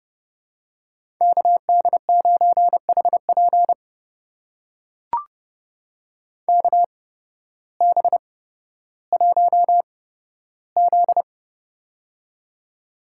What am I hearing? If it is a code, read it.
Morse code, heard as a keyed tone, about 31 wpm, spaced as for 14 wpm, 700 Hz. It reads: KD9HP KB1Z
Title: USA Call Signs - 8x Character Spacing - 30wpm